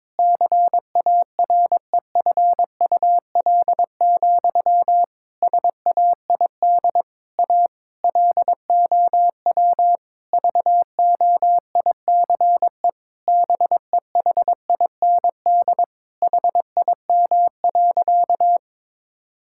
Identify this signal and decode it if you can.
Morse code, heard as a keyed tone, about 22 wpm, 700 Hz. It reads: CAREFUL, SAID A LOW VOICE BEHIND HIM.
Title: Sentences from Top 1600 Words 22wpm